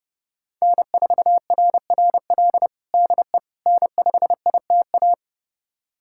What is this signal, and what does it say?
Morse code, heard as a keyed tone, about 30 wpm, 700 Hz. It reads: N4RRL DE N5ITA
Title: USA Callsign DE Callsign - 5 Characters - 30wpm